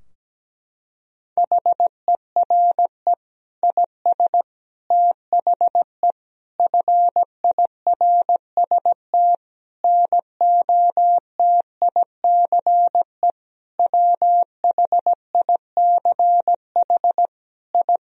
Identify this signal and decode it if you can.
Morse code, heard as a keyed tone, about 17 wpm, 700 Hz. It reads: HERE IS THE FIRST NOTICE WHICH I
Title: Sentences from Top 1000 Words - Review Part 2 - 17wpm